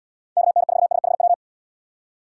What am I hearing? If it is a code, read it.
Morse code, heard as a keyed tone, about 74 wpm, 700 Hz. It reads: KN4SHW